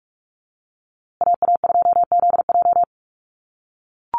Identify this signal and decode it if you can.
Morse code, heard as a keyed tone, about 44 wpm, 700 Hz. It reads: AA2ZJ
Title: USA Call Signs Repeated 2x 45wpm